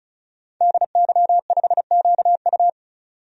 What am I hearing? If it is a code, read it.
Morse code, heard as a keyed tone, about 36 wpm, 700 Hz. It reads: DY5QU
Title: DXCC-100 Call Signs - 5 Characters - 35wpm